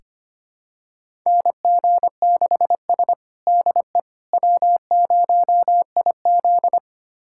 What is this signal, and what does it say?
Morse code, heard as a keyed tone, about 25 wpm, 700 Hz. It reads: NG6S DE W0IZ
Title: USA Callsign DE Callsign - 4 Characters - 25wpm